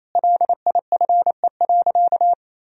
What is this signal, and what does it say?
Morse code, heard as a keyed tone, about 28 wpm, 700 Hz. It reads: LIFE.
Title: Sentences from Top 1000 Words - Review Part 3 - 28wpm